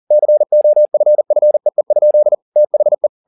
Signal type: Morse code